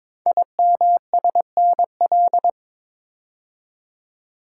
Morse code, heard as a keyed tone, about 22 wpm, 700 Hz. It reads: IMSNL
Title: Code Groups - 5 Letters - 22wpm